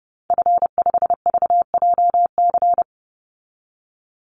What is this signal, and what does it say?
Morse code, heard as a keyed tone, about 30 wpm, 700 Hz. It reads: F5VJC
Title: Call Signs Repeated 2x - 30wpm